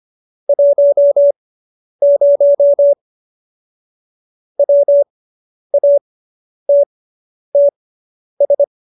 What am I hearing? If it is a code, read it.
Morse code, heard as a keyed tone, about 25 wpm, spaced as for 10 wpm, 570 Hz. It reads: 10 WATTS